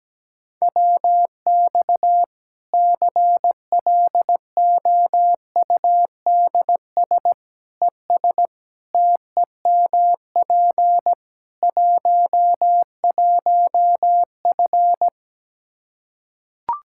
Morse code, heard as a keyed tone, about 17 wpm, 700 Hz. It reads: WX CLOUDS ES TEMP 11F
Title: Practice QSO - Exchange 2 - Info - 17wpm